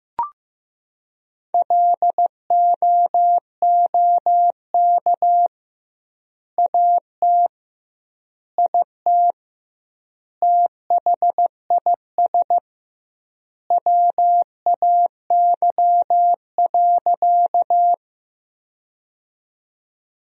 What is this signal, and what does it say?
Morse code, heard as a keyed tone, about 15 wpm, 700 Hz. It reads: LOOK AT IT THIS WAY.
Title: 2x Word-Spacing - Sentences from Top 100 Words - 15wpm